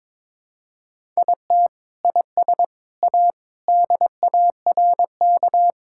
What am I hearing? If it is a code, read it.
Morse code, heard as a keyed tone, about 22 wpm, 700 Hz. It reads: IT IS A DARK
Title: Sentences from Top 1000 Words - Review Part 5 - 22wpm